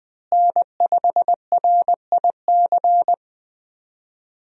Morse code, heard as a keyed tone, about 20 wpm, 700 Hz. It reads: N5RIC
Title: USA Call Signs - 5 Characters - Repeated 3x - 20wpm